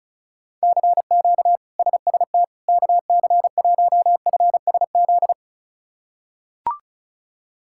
Morse code, heard as a keyed tone, about 34 wpm, 700 Hz. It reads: CQ SST KC1FSZ
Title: K1USN SST QSO Practice - 35wpm